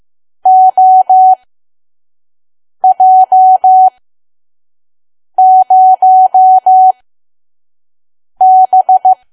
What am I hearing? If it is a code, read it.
Morse code, heard as a keyed tone, about 15 wpm, spaced as for 5 wpm, 750 Hz. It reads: OJ0B
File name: Archive / 5 WPM files / PM